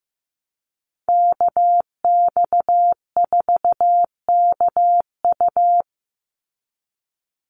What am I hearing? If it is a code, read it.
Morse code, heard as a keyed tone, about 15 wpm, 700 Hz. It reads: KX4KU
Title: Call Signs Repeated 2x - 15wpm